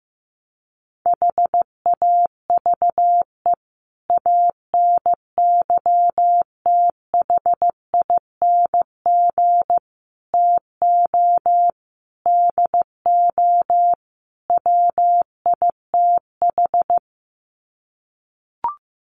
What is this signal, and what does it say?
Morse code, heard as a keyed tone, about 15 wpm, 700 Hz. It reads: HAVE ANYTHING TO DO WITH
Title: Sets of 5 Words 15wpm